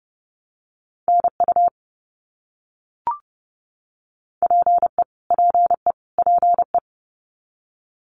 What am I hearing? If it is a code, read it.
Morse code, heard as a keyed tone, about 30 wpm, 700 Hz. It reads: NU PE PE PE
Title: Canadian Provinces and Territories - Repeated 3x - 30wpm